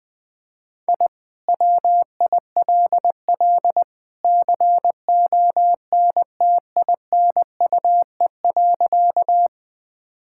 Morse code, heard as a keyed tone, about 20 wpm, 700 Hz. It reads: I WILL CONTINUE.
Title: Sentences from Top 1000 Words - Review Part 6 - 20wpm